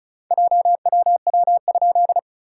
Morse code, heard as a keyed tone, about 36 wpm, 690 Hz. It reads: JWW?
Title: SOTA QSO Practice 35wpm